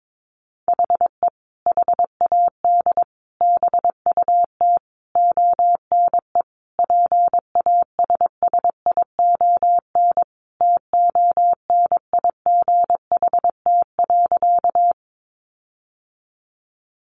Morse code, heard as a keyed tone, about 22 wpm, 700 Hz. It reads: HE HAD BUT ONE PASSION TONIGHT.
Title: Sentences from Top 1300 Words 22wpm